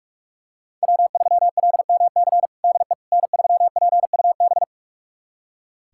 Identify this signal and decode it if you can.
Morse code, heard as a keyed tone, about 42 wpm, 700 Hz. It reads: W3LMC DE N3PUB
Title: USA Callsign DE Callsign - 5 Characters - 45wpm